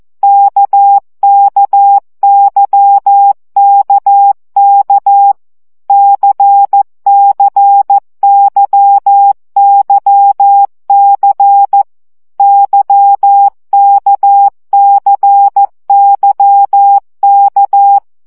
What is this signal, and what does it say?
Morse code, heard as a keyed tone, about 14 wpm, 800 Hz. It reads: KKYKK CCYYC YKCYK
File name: CW gyakorlas k c y 12 baud